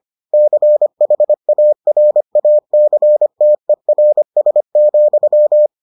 Morse code, heard as a keyed tone, about 25 wpm, 600 Hz. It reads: CHARACTERS,